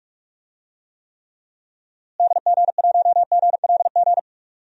Morse code, heard as a keyed tone, about 46 wpm, 700 Hz. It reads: DG1GLG
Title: DXCC-100 Call Signs - 6 Characters - Repeated 2x - 45wpm